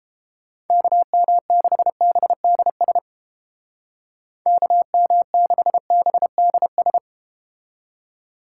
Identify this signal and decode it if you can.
Morse code, heard as a keyed tone, about 32 wpm, 700 Hz. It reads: KM6BDS KM6BDS